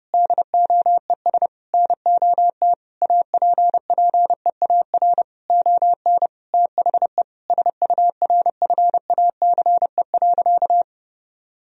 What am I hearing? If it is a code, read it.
Morse code, heard as a keyed tone, about 32 wpm, 700 Hz. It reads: DOES NOT APPEAR ON THE SURFACE.
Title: Sentences from Top 1000 Words - Review Part 3 - 30wpm